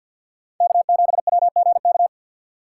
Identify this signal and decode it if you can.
Morse code, heard as a keyed tone, about 52 wpm, 700 Hz. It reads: K7WGK
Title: USA Call Signs - 5 Characters - 3x Character Spacing - 50wpm